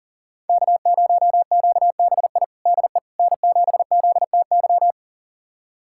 Morse code, heard as a keyed tone, about 40 wpm, 700 Hz. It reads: K0QBI DE N7ZTY
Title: USA Callsign DE Callsign - 5 Characters - 40wpm